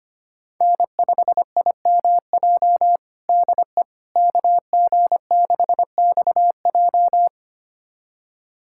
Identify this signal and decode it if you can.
Morse code, heard as a keyed tone, about 25 wpm, 700 Hz. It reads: N5IMJ DE KG6XJ
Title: USA Callsign DE Callsign - 5 Characters - 25wpm